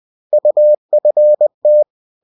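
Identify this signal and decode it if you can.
Morse code, heard as a keyed tone, about 20 wpm, 600 Hz. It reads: UFT
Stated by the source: Bulletin sent in CW.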